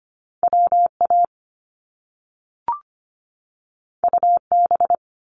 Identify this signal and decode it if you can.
Morse code, heard as a keyed tone, about 25 wpm, 700 Hz. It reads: WA UB
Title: Top Two Letter Combinations in Words 25wpm